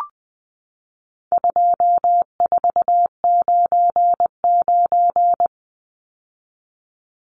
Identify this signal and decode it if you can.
Morse code, heard as a keyed tone, about 20 wpm, 700 Hz. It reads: 2499
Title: Numbers - 4-Digit 20wpm